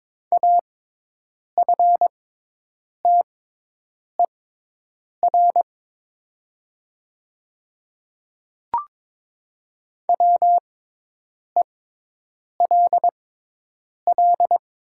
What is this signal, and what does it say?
Morse code, heard as a keyed tone, about 23 wpm, spaced as for 8 wpm, 700 Hz. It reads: AFTER WELL
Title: 6x Character Spacing - Top 200 Words - 22wpm